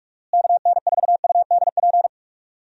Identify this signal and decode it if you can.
Morse code, heard as a keyed tone, about 46 wpm, 700 Hz. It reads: KN4UDP
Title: USA Call Signs - 6 Characters - Repeated 3x - 45wpm